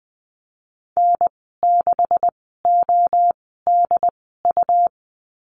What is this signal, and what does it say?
Morse code, heard as a keyed tone, about 20 wpm, spaced as for 15 wpm, 700 Hz. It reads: N6ODU
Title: 128 Lesson 21g - Number 6 - Call Signs 20wpm